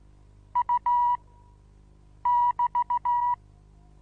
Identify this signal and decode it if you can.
Morse code, heard as a keyed tone, about 16 wpm, 970 Hz. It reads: U <BT>
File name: Onlinelehrgang / morsen / DJ4UF Morsen25